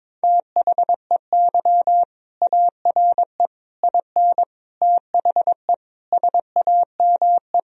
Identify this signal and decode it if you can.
Morse code, heard as a keyed tone, about 22 wpm, 700 Hz. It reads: THEY ARE IN THE SAME
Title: Sentences from Top 1000 Words - Review Part 2 - 22wpm